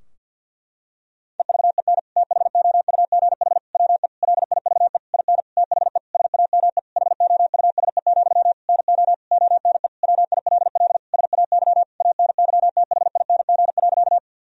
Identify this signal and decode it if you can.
Morse code, heard as a keyed tone, about 50 wpm, 700 Hz. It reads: EVEN THOUGH WE LIVE IN THE SAME HOUSE, NO ONE WILL SAY ANYTHING.
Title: Sentences from Top 1000 Words - Review Part 2 - 50wpm